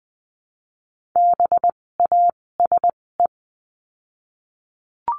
Morse code, heard as a keyed tone, about 20 wpm, spaced as for 16 wpm, 700 Hz. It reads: BASE
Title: 256 Lesson 42a - Words 12-20wpm